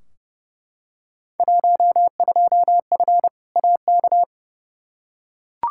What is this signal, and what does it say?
Morse code, heard as a keyed tone, about 32 wpm, 700 Hz. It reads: 12F AK
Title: Field Day Exchanges Encore - 30wpm